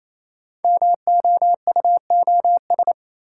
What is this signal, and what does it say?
Morse code, heard as a keyed tone, about 27 wpm, 700 Hz. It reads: MOUOS
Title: Code Groups - 5 Letters - 28wpm